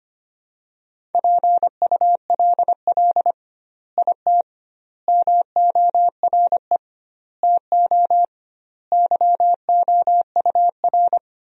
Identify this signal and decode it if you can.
Morse code, heard as a keyed tone, about 25 wpm, 700 Hz. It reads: PULL IT MORE TO YOUR
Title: 2x Word-Spacing - Sentences from Top 400 Words - 25wpm